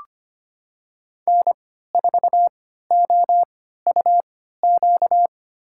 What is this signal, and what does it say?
Morse code, heard as a keyed tone, about 25 wpm, spaced as for 14 wpm, 700 Hz. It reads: N4OUQ
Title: USA Call Signs - 5 Characters - 3x Character Spacing - 25wpm